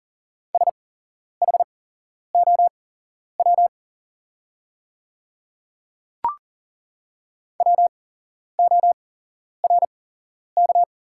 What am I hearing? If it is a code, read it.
Morse code, heard as a keyed tone, about 40 wpm, spaced as for 11 wpm, 700 Hz. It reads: SHOW WORK
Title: 8x Character Spacing - Top 200 Words - 40wpm